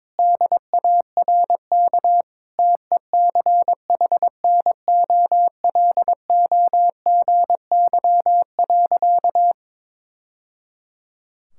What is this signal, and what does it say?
Morse code, heard as a keyed tone, about 21 wpm, 700 Hz. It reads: DARK TECHNOLOGY.
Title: Sentences from Top 1000 Words - Review Part 5 - 22wpm